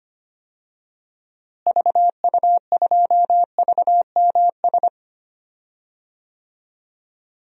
Morse code, heard as a keyed tone, about 25 wpm, 700 Hz. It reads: VU2VMS
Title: DXCC-100 Call Signs - 6 Characters - Repeated 2x - 25wpm